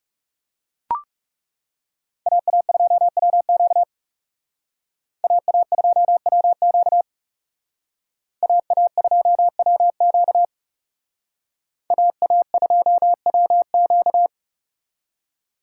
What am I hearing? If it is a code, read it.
Morse code, heard as a keyed tone, about 39 wpm, 700 Hz. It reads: AA2WQ AA2WQ AA2WQ AA2WQ